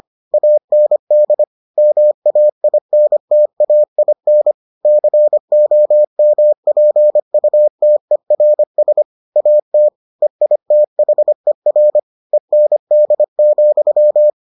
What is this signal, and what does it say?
Morse code, heard as a keyed tone, about 25 wpm, 600 Hz. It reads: AND MAINTAIN COMPUTERS AT EITHER END,